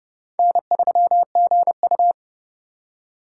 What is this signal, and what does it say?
Morse code, heard as a keyed tone, about 31 wpm, 700 Hz. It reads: N3GU
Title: USA Call Signs - 4 Characters - 2x Character Spacing - 30wpm